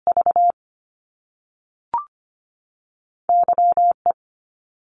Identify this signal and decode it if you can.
Morse code, heard as a keyed tone, about 25 wpm, 700 Hz. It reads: V Y E
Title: Top Two Letter Combinations in Words 25wpm